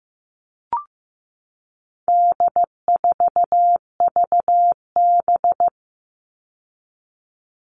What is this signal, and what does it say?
Morse code, heard as a keyed tone, about 15 wpm, 700 Hz. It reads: D4VB
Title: All Cut Numbers - 4-Digit 15wpm